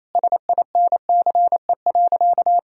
Morse code, heard as a keyed tone, about 28 wpm, 700 Hz. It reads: SINCE.
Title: Sentences from Top 400 Words and Abbreviations 28wpm